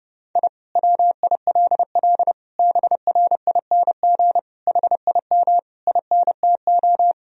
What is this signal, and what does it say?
Morse code, heard as a keyed tone, about 30 wpm, 700 Hz. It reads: I WILL BRING HIM INTO
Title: Sentences from Top 1000 Words - Review Part 3 - 30wpm